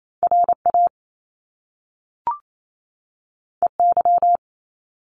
Morse code, heard as a keyed tone, about 28 wpm, 700 Hz. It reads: R A E Y
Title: Top Two Letter Combinations in Words 28wpm